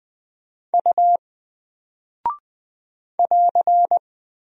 Morse code, heard as a keyed tone, about 20 wpm, 700 Hz. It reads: U <AR>
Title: Single Character (Letter:Number:Punctuation:Prosign) 20wpm